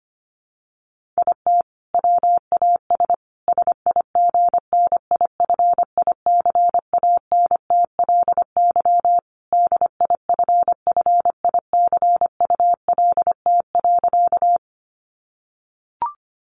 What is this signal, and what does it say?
Morse code, heard as a keyed tone, about 25 wpm, 700 Hz. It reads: IT WAS SIGNIFICANTLY DIFFICULT.
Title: Sentences from Top 1500 Words 25wpm